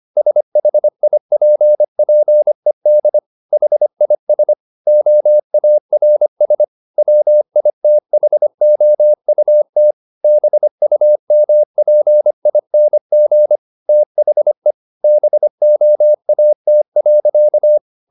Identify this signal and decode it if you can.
Morse code, heard as a keyed tone, about 25 wpm, 600 Hz. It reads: SHIPPED HIS OARS WITHOUT BUMPING THE BOAT.
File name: the old man and the sea part 1 25wpm